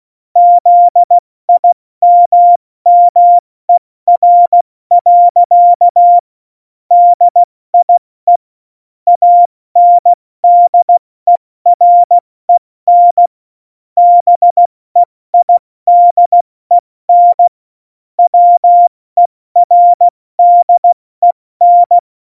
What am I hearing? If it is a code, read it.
Morse code, heard as a keyed tone, about 16 wpm, 700 Hz. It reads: ZIMMER. DIE ANDEREN BEIDEN WERDEN
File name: Morsekurs / Witze / Witz 19. Januar 2021, 70 F80